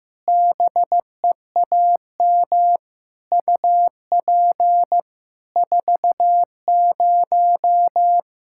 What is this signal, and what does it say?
Morse code, heard as a keyed tone, about 15 wpm, 700 Hz. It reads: BEAM UP 40
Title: Practice QSO - Exchange 2 - Info - 15wpm